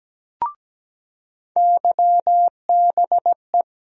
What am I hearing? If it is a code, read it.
Morse code, heard as a keyed tone, about 17 wpm, 700 Hz. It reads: YBE
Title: USA Call Sign Postfixes - 3 Letters 17wpm